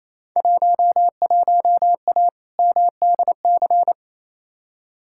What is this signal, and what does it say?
Morse code, heard as a keyed tone, about 28 wpm, 700 Hz. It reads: 11A MDC
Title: Field Day Exchanges Encore - 28wpm